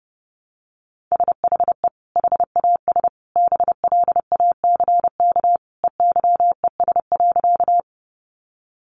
Morse code, heard as a keyed tone, about 30 wpm, 700 Hz. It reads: SHE HAS BLACK EYES.
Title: Easy Phrases 30wpm